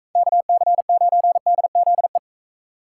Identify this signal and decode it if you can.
Morse code, heard as a keyed tone, about 44 wpm, 700 Hz. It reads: KC9DZE